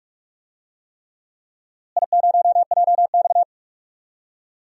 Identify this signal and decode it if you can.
Morse code, heard as a keyed tone, about 44 wpm, 700 Hz. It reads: I0JX
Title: DXCC-100 Call Signs - 4 Characters - Repeated 3x - 45wpm